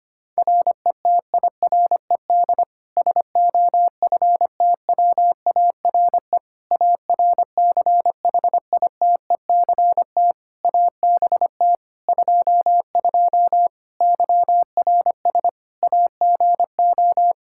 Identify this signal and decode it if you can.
Morse code, heard as a keyed tone, about 25 wpm, 700 Hz. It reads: RETIRED SOFTWARE ARCHITECT ABT 22 YRS AGO
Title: Practice QSO - Exchange 3 - More Info - 25wpm